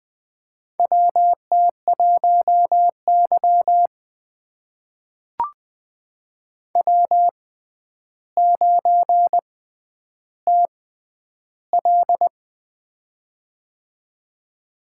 Morse code, heard as a keyed tone, about 20 wpm, spaced as for 10 wpm, 700 Hz. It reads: WT1Y W9TL
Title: USA Call Signs - 4 Characters - 6x Character Spacing - 20wpm